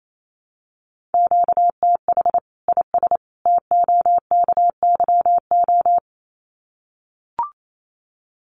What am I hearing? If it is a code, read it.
Morse code, heard as a keyed tone, about 28 wpm, 700 Hz. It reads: QTH IS TOKYO
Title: Common QSO Phrases 28wpm